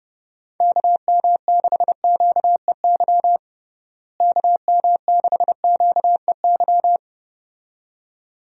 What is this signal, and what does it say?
Morse code, heard as a keyed tone, about 32 wpm, 700 Hz. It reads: KM6QEY KM6QEY
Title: USA Call Signs - 6 Characters - Repeated 2x - 30wpm